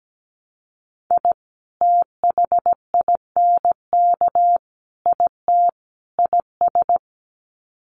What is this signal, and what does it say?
Morse code, heard as a keyed tone, about 17 wpm, 700 Hz. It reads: I THINK IT IS
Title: Sets of 4 Words 17wpm